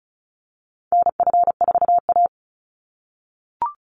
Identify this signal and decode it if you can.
Morse code, heard as a keyed tone, about 34 wpm, 700 Hz. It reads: NF4A
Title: Call Signs - 35wpm